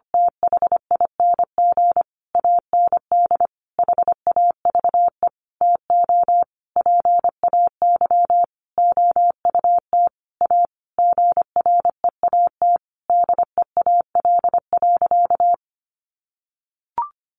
Morse code, heard as a keyed tone, about 25 wpm, 700 Hz. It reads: THING AND HAVE TO PAY OUT A GREAT DEAL.
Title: Sentences from Top 1500 Words 25wpm